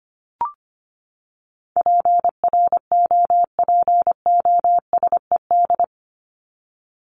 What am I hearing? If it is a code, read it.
Morse code, heard as a keyed tone, about 25 wpm, 700 Hz. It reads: PROPOSED
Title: ED Suffix - Top 100 Words with Base Words 25wpm